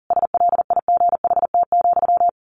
Morse code, heard as a keyed tone, about 40 wpm, 700 Hz. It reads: SLIGHT,
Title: A Sound Of Thunder 40wpm